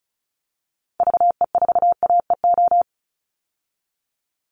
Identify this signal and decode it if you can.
Morse code, heard as a keyed tone, about 34 wpm, 700 Hz. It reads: VE4AEO
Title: Call Signs Repeated 2x - 35wpm